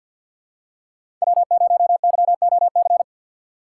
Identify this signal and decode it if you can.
Morse code, heard as a keyed tone, about 53 wpm, 700 Hz. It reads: W0YOC
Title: USA Call Signs - 5 Characters - 4x Character Spacing - 50wpm